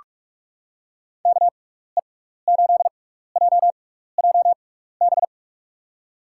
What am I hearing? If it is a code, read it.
Morse code, heard as a keyed tone, about 43 wpm, spaced as for 15 wpm, 700 Hz. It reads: KE8JJB
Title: USA Call Signs - 6 Characters - 6x Character Spacing - 45wpm